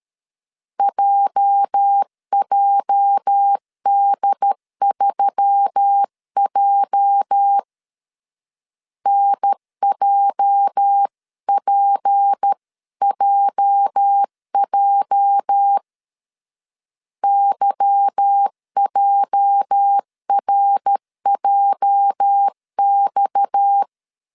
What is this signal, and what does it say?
Morse code, heard as a keyed tone, about 13 wpm, 800 Hz. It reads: JJD3J NJPJJ YJRJX
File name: LEZIONI / 8 YJ45 / 8 YJ 12wpm 2235